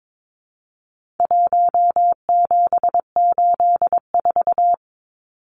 Morse code, heard as a keyed tone, about 22 wpm, 700 Hz. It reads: 1784
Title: Numbers - 4-Digit 22wpm